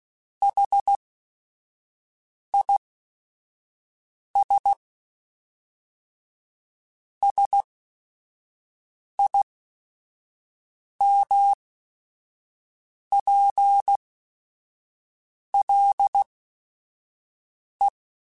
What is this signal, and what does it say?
Morse code, heard as a keyed tone, about 16 wpm, spaced as for 5 wpm, 800 Hz. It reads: HIS SIMPLE
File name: Technical Subjects / Technical Subjects-mp3s / OR Gate @5